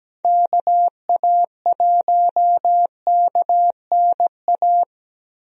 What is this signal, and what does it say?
Morse code, heard as a keyed tone, about 17 wpm, 700 Hz. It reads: KA1KNA
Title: USA Call Signs - 6 Characters - 17wpm